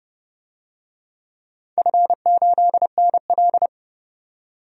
Morse code, heard as a keyed tone, about 29 wpm, 700 Hz. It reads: F8NL
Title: DXCC-100 Call Signs - 4 Characters - 30wpm